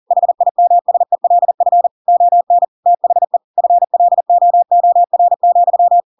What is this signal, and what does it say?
Morse code, heard as a keyed tone, about 40 wpm, 700 Hz. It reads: HIMSELF ON THE FLOOR,
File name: Pinocchio / Pinocchio ch4 40 40 ik1hge